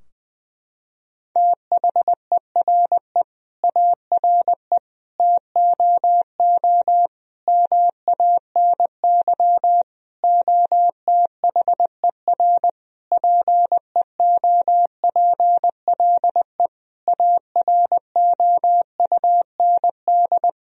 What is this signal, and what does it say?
Morse code, heard as a keyed tone, about 20 wpm, 700 Hz. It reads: THERE ARE TOO MANY OTHER PEOPLE AROUND
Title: Sentences from Top 1000 Words - Review Part 3 - 20wpm